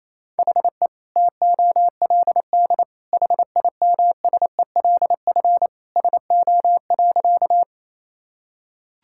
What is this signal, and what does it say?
Morse code, heard as a keyed tone, about 28 wpm, 700 Hz. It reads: HE TOLD HIMSELF SO.
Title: Sentences from Top 1600 Words 28wpm